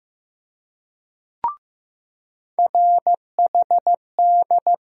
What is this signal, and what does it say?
Morse code, heard as a keyed tone, about 15 wpm, 700 Hz. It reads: RHD
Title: USA Call Sign Postfixes - 3 Letters 15wpm